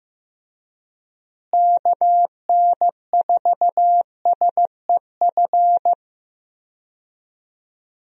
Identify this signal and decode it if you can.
Morse code, heard as a keyed tone, about 15 wpm, 700 Hz. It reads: KN4SEF
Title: USA Call Signs - 6 Characters - Repeated 2x - 15wpm